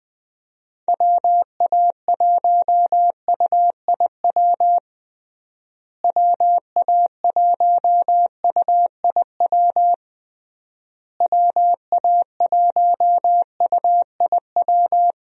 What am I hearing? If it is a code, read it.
Morse code, heard as a keyed tone, about 21 wpm, 700 Hz. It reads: WA1UIW WA1UIW WA1UIW
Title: USA Call Signs - 6 Characters - Repeated 3x - 20wpm